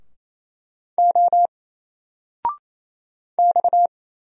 Morse code, heard as a keyed tone, about 28 wpm, 700 Hz. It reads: O X
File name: Single-Letter-Number-28wpm